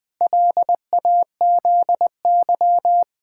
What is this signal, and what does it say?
Morse code, heard as a keyed tone, about 21 wpm, 700 Hz. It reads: LAZY
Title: Sets of 8 Words - 20wpm